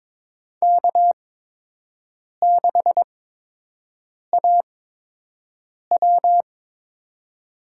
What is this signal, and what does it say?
Morse code, heard as a keyed tone, about 22 wpm, spaced as for 6 wpm, 700 Hz. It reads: K6AW
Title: USA Call Signs - 4 Characters - 8x Character Spacing - 22wpm